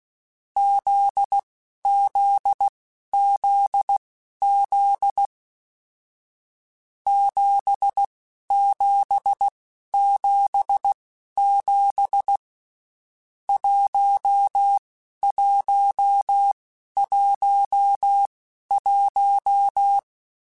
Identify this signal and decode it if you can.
Morse code, heard as a keyed tone, about 16 wpm, spaced as for 11 wpm, 800 Hz. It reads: ZZZZ 7777 1111
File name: Basic Sets / Basic Letter Groups Collective7 / JZ17@10